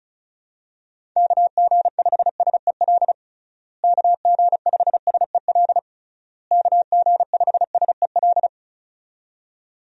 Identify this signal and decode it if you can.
Morse code, heard as a keyed tone, about 36 wpm, 700 Hz. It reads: KG5SEL KG5SEL KG5SEL
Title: USA Call Signs - 6 Characters - Repeated 3x - 35wpm